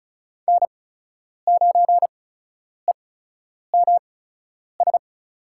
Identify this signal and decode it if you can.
Morse code, heard as a keyed tone, about 36 wpm, spaced as for 10 wpm, 700 Hz. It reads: N9EMS
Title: USA Call Signs - 8x Character Spacing - 35wpm